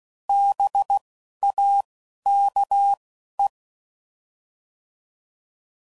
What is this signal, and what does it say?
Morse code, heard as a keyed tone, about 16 wpm, 800 Hz. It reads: B A K E